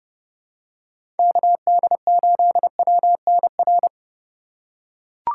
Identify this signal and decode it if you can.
Morse code, heard as a keyed tone, about 30 wpm, 700 Hz. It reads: KD8WNR
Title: USA Call Signs - 6 Characters - 2x Character Spacing - 30wpm